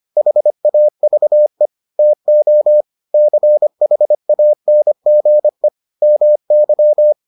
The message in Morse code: HAVE TO CHANGE MY

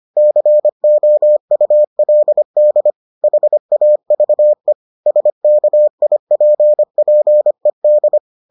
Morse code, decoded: COULD HAVE SKIPPED

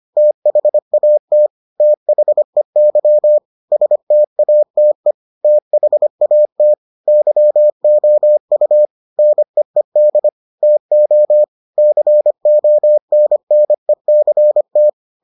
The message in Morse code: THAT THEY STATE THAT YOU NEED TO CONNECT

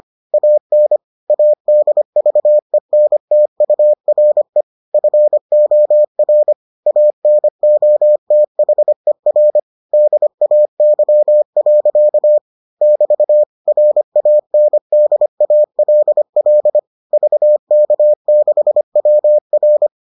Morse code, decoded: AN ADVENTURE FOR ANOTHER DAY. <BT> RANDALL VK6WR

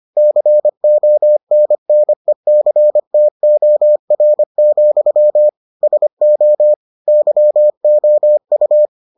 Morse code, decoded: CONNECTOR, SO YOU

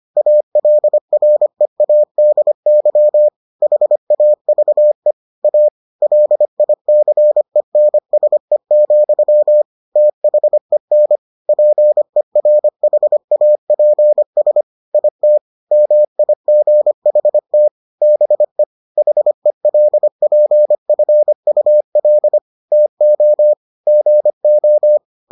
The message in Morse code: ALREADY HAVE A LICENSE, THEN PERHAPS IT MIGHT BE HELPFUL TO GO

